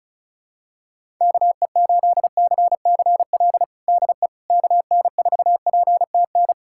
Morse code, heard as a keyed tone, about 36 wpm, 700 Hz. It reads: KE8CCL DE KN4PTN
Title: USA Callsign DE Callsign - 6 Characters - 35wpm